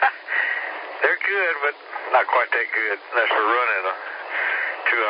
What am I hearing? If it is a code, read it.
Emergency Alert System (SAME) tones, heard as harsh buzzing bursts, with speech. It is heard from 1.0 to 4.7 s.